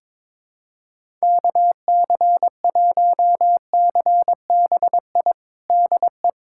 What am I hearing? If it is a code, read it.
Morse code, heard as a keyed tone, about 22 wpm, 700 Hz. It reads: KC1CBI DE
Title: USA Callsign DE Callsign - 6 Characters - 22wpm